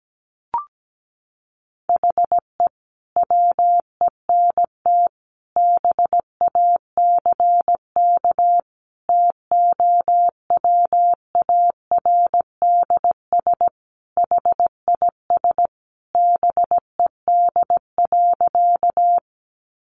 Morse code, heard as a keyed tone, about 17 wpm, 700 Hz. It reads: HE WENT BACK TOWARDS HIS BED.
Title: Sentences from Top 1100 Words 17wpm